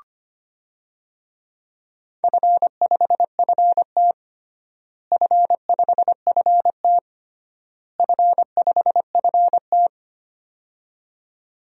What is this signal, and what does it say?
Morse code, heard as a keyed tone, about 25 wpm, 700 Hz. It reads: F5FT F5FT F5FT
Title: DXCC-100 Call Signs - 4 Characters - Repeated 3x - 25wpm